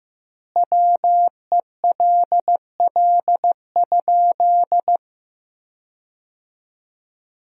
Morse code, heard as a keyed tone, about 15 wpm, 700 Hz. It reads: WELL?
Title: 2x Word-Spacing - Sentences from Top 400 Words - 15wpm